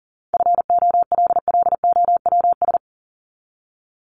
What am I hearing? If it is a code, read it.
Morse code, heard as a keyed tone, about 38 wpm, 700 Hz. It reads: FOLLOWS
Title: ES:S Suffix - Top 100 Words with Base Words 40wpm